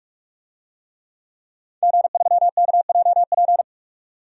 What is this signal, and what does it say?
Morse code, heard as a keyed tone, about 46 wpm, 700 Hz. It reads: G3KJP
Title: DXCC-100 Call Signs - 5 Characters - 45wpm